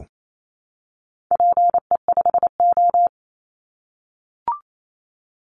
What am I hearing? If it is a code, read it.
Morse code, heard as a keyed tone, about 28 wpm, 700 Hz. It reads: PE5O
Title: Call Signs - 28wpm